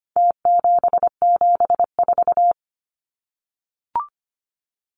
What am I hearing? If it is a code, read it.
Morse code, heard as a keyed tone, about 25 wpm, 700 Hz. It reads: T774
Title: Numbers - Contest Style - 4-Digit 25wpm